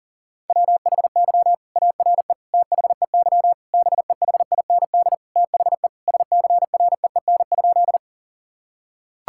Morse code, heard as a keyed tone, about 40 wpm, 700 Hz. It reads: WHY ARE THEY BEHIND THE SCREEN?
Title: Sentences from Top 1000 Words - Review Part 6 - 40wpm